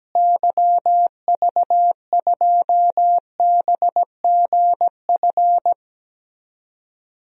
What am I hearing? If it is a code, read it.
Morse code, heard as a keyed tone, about 17 wpm, 700 Hz. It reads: YV2BGF
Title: DXCC-100 Call Signs - 6 Characters - Repeated 2x - 17wpm